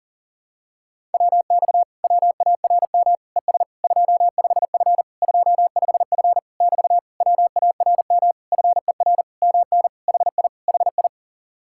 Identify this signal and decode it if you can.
Morse code, heard as a keyed tone, about 38 wpm, 700 Hz. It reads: WX WARM ES 25F 25F <BT> WARM FER MN HI HI
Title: Speed-Racing Common QSO Phrases Encore - 40wpm